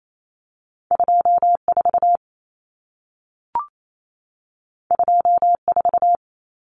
Morse code, heard as a keyed tone, about 28 wpm, 700 Hz. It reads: 24 24
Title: Numbers - 2-Digit 28wpm